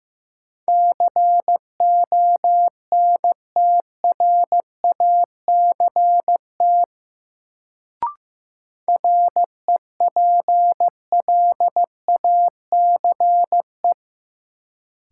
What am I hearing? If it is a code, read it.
Morse code, heard as a keyed tone, about 15 wpm, 700 Hz. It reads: CONTRACT REPLACE
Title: Sentences from Top 1400 Words - New Word Drill - 15wpm